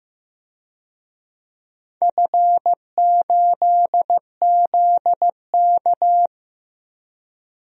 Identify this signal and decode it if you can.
Morse code, heard as a keyed tone, about 15 wpm, 700 Hz. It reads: F8ZK
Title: DXCC-100 Call Signs - 4 Characters - 15wpm